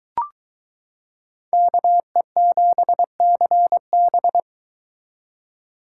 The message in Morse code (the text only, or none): KE7CB